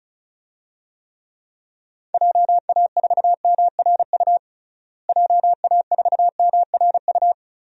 Morse code, heard as a keyed tone, about 34 wpm, 700 Hz. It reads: JA4MRU JA4MRU
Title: DXCC-100 Call Signs - 6 Characters - Repeated 2x - 35wpm